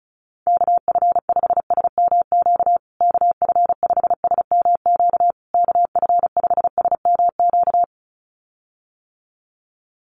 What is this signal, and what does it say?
Morse code, heard as a keyed tone, about 34 wpm, 700 Hz. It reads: KF5SMQ KF5SMQ KF5SMQ
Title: USA Call Signs Repeated 3x 35wpm